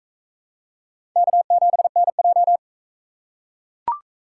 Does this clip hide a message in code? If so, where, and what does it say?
Yes: Morse code from 1.2 to 2.6 s, heard as a keyed tone, about 40 wpm, 700 Hz. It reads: K7NJ